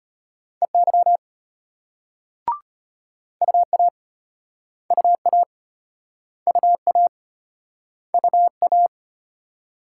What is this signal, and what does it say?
Morse code, heard as a keyed tone, about 35 wpm, 700 Hz. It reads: EY UA UA UA UA